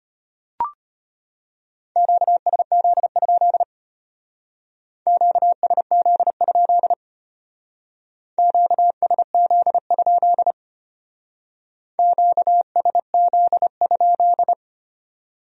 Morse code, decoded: QSZ? QSZ? QSZ? QSZ?

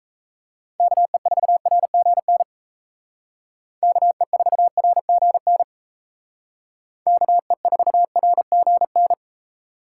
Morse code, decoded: KE4RGN KE4RGN KE4RGN

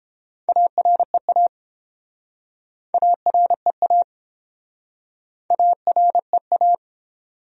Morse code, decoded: AREA AREA AREA

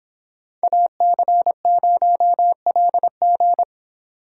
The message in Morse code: AC0LG

